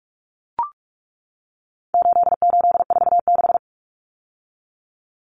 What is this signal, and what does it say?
Morse code, heard as a keyed tone, about 44 wpm, 700 Hz. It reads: 8846
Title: Numbers - 4-Digit 45wpm